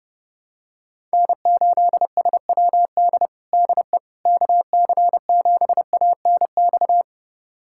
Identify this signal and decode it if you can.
Morse code, heard as a keyed tone, about 32 wpm, 700 Hz. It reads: N8SWD DE KC7ANX
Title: USA Callsign DE Callsign 30wpm